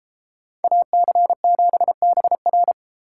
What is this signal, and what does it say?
Morse code, heard as a keyed tone, about 31 wpm, 700 Hz. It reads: AC7BR